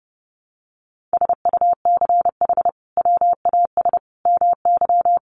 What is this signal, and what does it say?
Morse code, heard as a keyed tone, about 30 wpm, 700 Hz. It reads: SUCH WAS MY
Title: Sentences from Top 800 Words 30wpm